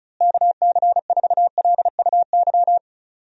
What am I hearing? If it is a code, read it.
Morse code, heard as a keyed tone, about 36 wpm, 700 Hz. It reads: KC4LUY
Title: USA Call Signs - 6 Characters - Repeated 3x - 35wpm